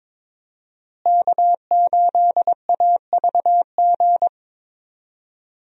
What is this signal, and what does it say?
Morse code, heard as a keyed tone, about 22 wpm, 700 Hz. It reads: K8AVG
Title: USA Call Signs - 5 Characters - Repeated 3x - 22wpm